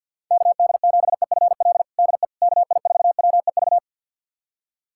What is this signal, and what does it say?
Morse code, heard as a keyed tone, about 51 wpm, 700 Hz. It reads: KD7EFL DE KI4WEV
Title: USA Callsign DE Callsign - 6 Characters - 50wpm